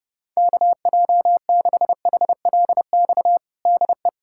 Morse code, heard as a keyed tone, about 30 wpm, 700 Hz. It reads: KJ6HLX DE
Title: USA Callsign DE Callsign - 6 Characters - 30wpm